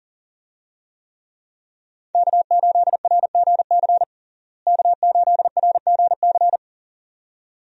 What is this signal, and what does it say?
Morse code, heard as a keyed tone, about 40 wpm, 700 Hz. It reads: K8RGC K8RGC
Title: USA Call Signs - 5 Characters - Repeated 2x - 40wpm